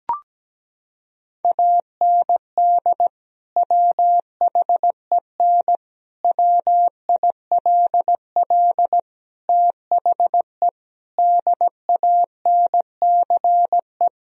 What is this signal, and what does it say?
Morse code, heard as a keyed tone, about 17 wpm, 700 Hz. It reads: AND WHEN WILL THE DANCE
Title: Sentences from Top 1000 Words - Review Part 4 - 17wpm